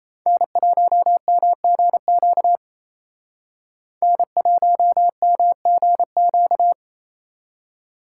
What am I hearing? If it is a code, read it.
Morse code, heard as a keyed tone, about 34 wpm, 700 Hz. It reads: N1MGQ N1MGQ
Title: Speed Racing - USA 1x3 Call Signs 28wpm